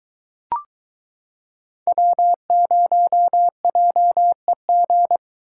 Morse code, heard as a keyed tone, about 22 wpm, 700 Hz. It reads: W0JEG